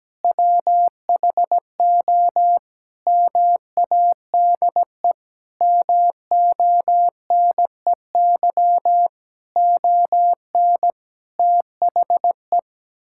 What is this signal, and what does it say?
Morse code, heard as a keyed tone, about 17 wpm, 700 Hz. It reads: WHO MADE MONEY ON THE
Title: Sentences from Top 1000 Words - Review Part 6 - 17wpm